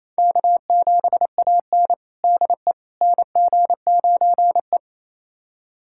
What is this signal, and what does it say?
Morse code, heard as a keyed tone, about 28 wpm, 700 Hz. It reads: K7AN DE NG9E
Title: USA Callsign DE Callsign - 4 Characters - 28wpm